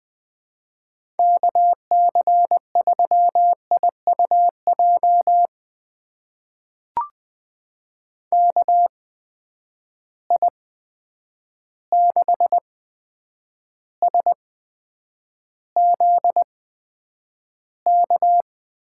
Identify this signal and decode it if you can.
Morse code, heard as a keyed tone, about 20 wpm, spaced as for 9 wpm, 700 Hz. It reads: KC3IUJ KI6SZK
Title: USA Call Signs - 6 Characters - 8x Character Spacing - 20wpm